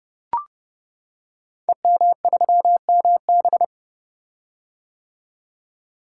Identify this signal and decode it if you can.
Morse code, heard as a keyed tone, about 30 wpm, 700 Hz. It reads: EM3MB
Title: Code Groups - 5 Letter:Numbers - 30wpm